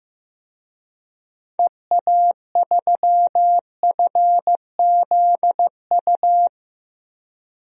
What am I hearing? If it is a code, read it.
Morse code, heard as a keyed tone, about 15 wpm, 690 Hz. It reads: EA3FZU
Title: DXCC-100 Call Signs - 6 Characters - 15wpm